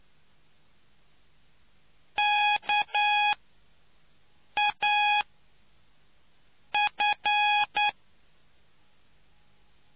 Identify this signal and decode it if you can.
Morse code, heard as a keyed tone, about 9 wpm, 830 Hz. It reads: K A F